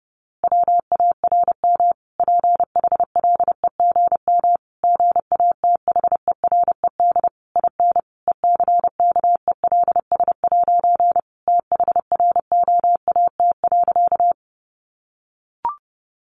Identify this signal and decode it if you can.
Morse code, heard as a keyed tone, about 30 wpm, 700 Hz. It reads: WARM PHLEGM GATHERED IN ECKELS' THROAT.
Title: A Sound Of Thunder 30wpm